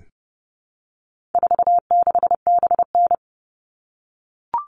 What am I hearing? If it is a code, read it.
Morse code, heard as a keyed tone, about 30 wpm, 700 Hz. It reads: 46BN
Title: All Cut Numbers - 4-Digit 30wpm